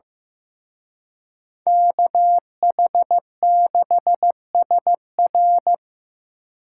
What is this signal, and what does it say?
Morse code, heard as a keyed tone, about 18 wpm, 700 Hz. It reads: KH6SR
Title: Speed Racing - USA 2x2 Call Signs 15wpm